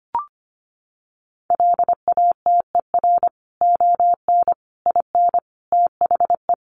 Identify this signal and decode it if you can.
Morse code, heard as a keyed tone, about 25 wpm, 700 Hz. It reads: LATER ON IN THE
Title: Sentences from Top 700 Words 25wpm